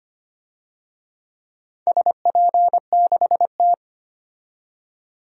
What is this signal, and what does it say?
Morse code, heard as a keyed tone, about 25 wpm, 700 Hz. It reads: SP6T
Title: DXCC-100 Call Signs - 4 Characters - 25wpm